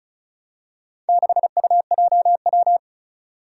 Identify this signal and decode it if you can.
Morse code, heard as a keyed tone, about 36 wpm, 700 Hz. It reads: 6UJW
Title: Four Challenging Letter-Number Combinations 35wpm